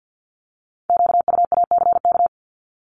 Morse code, heard as a keyed tone, about 50 wpm, 690 Hz. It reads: QUACK
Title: Multi-letter Phonemes 50wpm